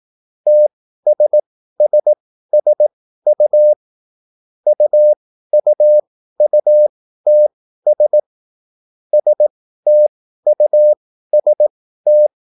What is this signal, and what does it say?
Morse code, heard as a keyed tone, about 18 wpm, spaced as for 13 wpm, 600 Hz. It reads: TSSSU UUUTS STUST
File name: QSL / CW lesson